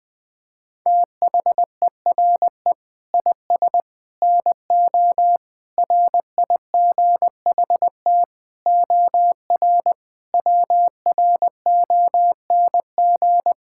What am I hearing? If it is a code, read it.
Morse code, heard as a keyed tone, about 21 wpm, 700 Hz. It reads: THERE IS NO RIGHT OR WRONG
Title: Sets of 8 Words - 20wpm